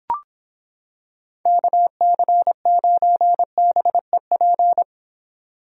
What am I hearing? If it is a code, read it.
Morse code, heard as a keyed tone, about 26 wpm, 700 Hz. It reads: KC9BEP